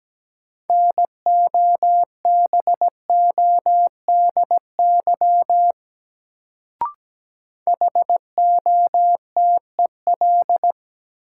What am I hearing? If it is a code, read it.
Morse code, heard as a keyed tone, about 17 wpm, 700 Hz. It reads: NOBODY HOTEL
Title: Sentences from Top 1600 Words - New Word Drill Alt - 17wpm